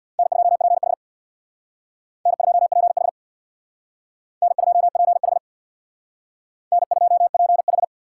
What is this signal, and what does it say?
Morse code, heard as a keyed tone, about 62 wpm, 700 Hz. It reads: N2PH N2PH N2PH N2PH
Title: Speed Racing - USA 1x2 Call Signs 50wpm